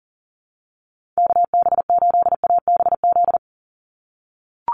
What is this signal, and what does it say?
Morse code, heard as a keyed tone, about 40 wpm, 700 Hz. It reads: KB8ABZ
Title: USA Call Signs 40wpm